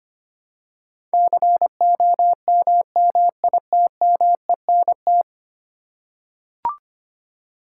Morse code, heard as a keyed tone, about 25 wpm, 700 Hz. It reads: COMMITMENT
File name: Ten Letter Words - 25wpm